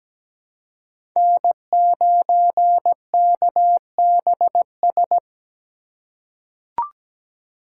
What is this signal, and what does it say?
Morse code, heard as a keyed tone, about 17 wpm, 700 Hz. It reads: N9KBS
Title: USA Call Signs - 5 Characters - 3x Character Spacing - 17wpm